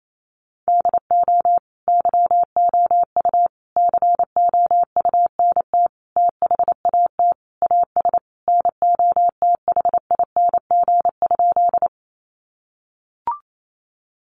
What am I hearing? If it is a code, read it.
Morse code, heard as a keyed tone, about 28 wpm, 700 Hz. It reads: DO YOU COUNT THAT AS NOTHING?
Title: Sentences from Top 1500 Words 28wpm